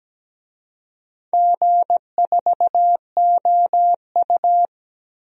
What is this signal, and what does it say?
Morse code, heard as a keyed tone, about 17 wpm, 700 Hz. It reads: G4OU
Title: DXCC-100 Call Signs - 4 Characters - Repeated 3x - 17wpm